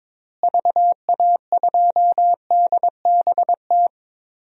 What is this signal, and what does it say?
Morse code, heard as a keyed tone, about 21 wpm, 700 Hz. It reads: VA2DBT
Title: DXCC-100 Call Signs - 6 Characters - Repeated 2x - 22wpm